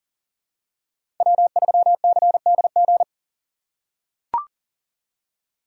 Morse code, heard as a keyed tone, about 40 wpm, 700 Hz. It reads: W3CDG
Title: USA Call Signs - 2x Character Spacing - 40wpm